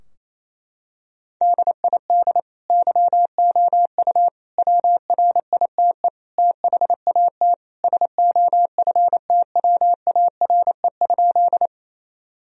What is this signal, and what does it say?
Morse code, heard as a keyed tone, about 28 wpm, 700 Hz. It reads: DID YOU WRITE THAT SOFTWARE?
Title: Sentences from Top 1000 Words - Review Part 3 - 28wpm